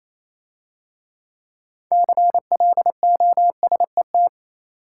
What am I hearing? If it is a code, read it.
Morse code, heard as a keyed tone, about 28 wpm, 700 Hz. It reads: CLOSET